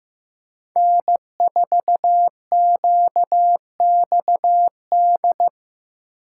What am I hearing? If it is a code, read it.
Morse code, heard as a keyed tone, about 15 wpm, 700 Hz. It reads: N4QXD
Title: USA Call Signs - 5 Characters - 3x Character Spacing - 15wpm